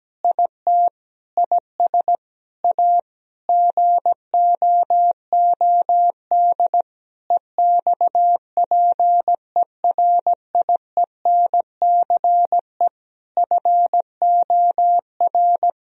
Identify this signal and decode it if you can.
Morse code, heard as a keyed tone, about 17 wpm, 700 Hz. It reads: IT IS A GOOD EXPERIENCE FOR
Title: Sentences from Top 1000 Words - Review Part 4 - 17wpm